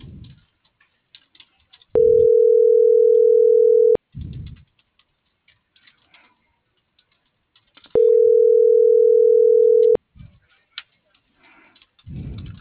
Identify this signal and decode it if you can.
Ringback tone (North America), 440 + 480 Hz.